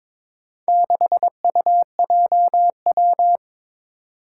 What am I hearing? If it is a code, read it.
Morse code, heard as a keyed tone, about 22 wpm, 700 Hz. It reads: 6UJW
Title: Four Challenging Letter-Number Combinations 22wpm